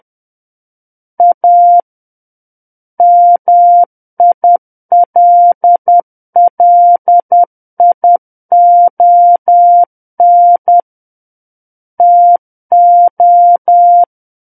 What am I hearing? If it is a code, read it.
Morse code, heard as a keyed tone, about 10 wpm, 700 Hz. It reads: A MILLION TO
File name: Book One / 01-Chapter 1 - 10 WPM